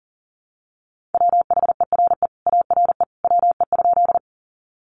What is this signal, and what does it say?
Morse code, heard as a keyed tone, about 40 wpm, 700 Hz. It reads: WHERE ARE WE?
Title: Sentences from Top 200 Words 40wpm